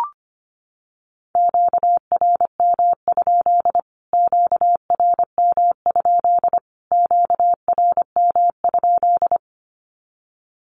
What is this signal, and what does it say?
Morse code, heard as a keyed tone, about 25 wpm, 700 Hz. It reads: QRM? QRM? QRM?
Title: Q-Codes Repeated 3x 25wpm